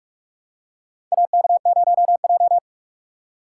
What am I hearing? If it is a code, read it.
Morse code, heard as a keyed tone, about 46 wpm, 700 Hz. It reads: AK0J
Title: USA Call Signs - 4 Characters - 3x Character Spacing - 45wpm